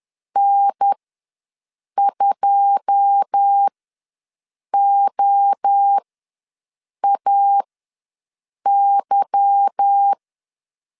Morse code, heard as a keyed tone, about 10 wpm, spaced as for 6 wpm, 800 Hz. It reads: N2OAY
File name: LEZIONI / 9 28 ws 10wpm 4124